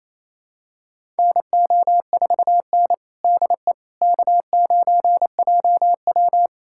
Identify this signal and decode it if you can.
Morse code, heard as a keyed tone, about 28 wpm, 700 Hz. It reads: NO4N DE K9JW
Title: USA Callsign DE Callsign - 4 Characters - 28wpm